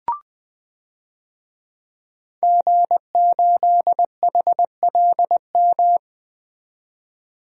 Morse code, heard as a keyed tone, about 20 wpm, 700 Hz. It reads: G8HLM
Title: DXCC-100 Call Signs - 5 Characters - 20wpm